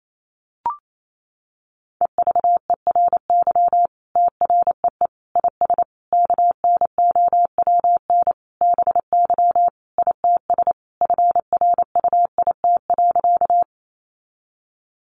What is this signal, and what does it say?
Morse code, heard as a keyed tone, about 28 wpm, 700 Hz. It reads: EVERY TREE IS KNOWN BY ITS FRUIT.
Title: Sentences from Top 1300 Words 28wpm